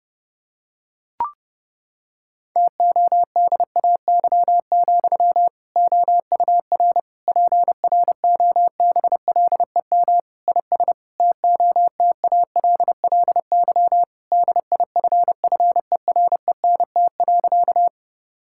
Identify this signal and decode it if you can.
Morse code, heard as a keyed tone, about 30 wpm, 700 Hz. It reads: TODAY, OUR PROBLEM IS TOTALLY DIFFERENT.
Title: Sentences from Top 1600 Words 30wpm